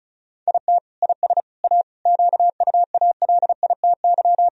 Morse code, heard as a keyed tone, about 35 wpm, 700 Hz. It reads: IT IS A QUALITY
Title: Sentences from Top 1000 Words - Review Part 4 - 35wpm